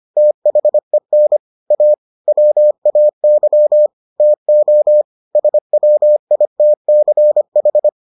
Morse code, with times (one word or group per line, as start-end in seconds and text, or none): 0.2-1.4 THEN
1.7-1.9 A
2.3-3.9 WAY
4.2-5.0 TO
5.4-7.9 SWITCH